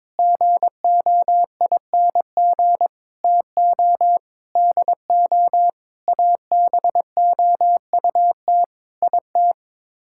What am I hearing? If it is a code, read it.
Morse code, heard as a keyed tone, about 22 wpm, 700 Hz. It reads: GOING TO DO ABOUT IT
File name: practice / 22wpm / Sets of 8 Words - 22wpm